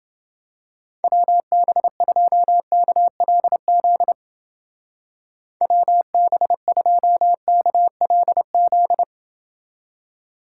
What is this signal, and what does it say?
Morse code, heard as a keyed tone, about 28 wpm, 700 Hz. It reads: WB2KLZ WB2KLZ